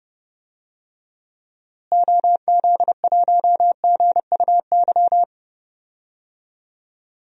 Morse code, heard as a keyed tone, about 30 wpm, 700 Hz. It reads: OZ1GUY
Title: DXCC-100 Call Signs - 6 Characters - Repeated 3x - 30wpm